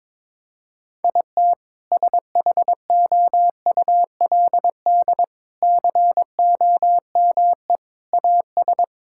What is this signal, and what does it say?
Morse code, heard as a keyed tone, about 22 wpm, 700 Hz. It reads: IT SHOULD COME AS
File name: practice / 22wpm / Sets of 8 Words - 22wpm